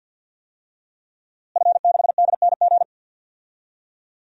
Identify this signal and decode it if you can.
Morse code, heard as a keyed tone, about 46 wpm, 700 Hz. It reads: F6DNG
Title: DXCC-100 Call Signs - 5 Characters - Repeated 3x - 50wpm